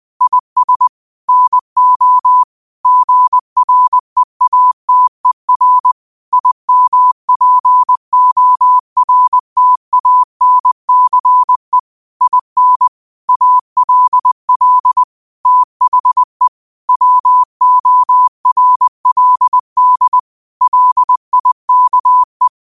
Morse code, heard as a keyed tone, about 20 wpm, 1000 Hz. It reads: IS NO GREATER IMPORTANCE IN ALL THE WORLD LIKE